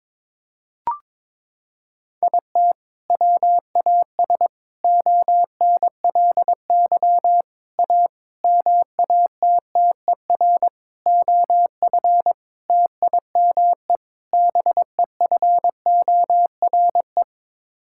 Morse code, heard as a keyed tone, about 21 wpm, 700 Hz. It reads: IT WAS ONLY A MATTER OF TIME BEFORE
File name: practice / 22wpm / Sets of 8 Words - 22wpm